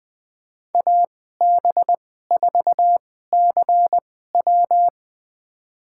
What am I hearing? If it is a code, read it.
Morse code, heard as a keyed tone, about 20 wpm, spaced as for 14 wpm, 700 Hz. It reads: AB4CW
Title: USA Call Signs - 2x Character Spacing - 20wpm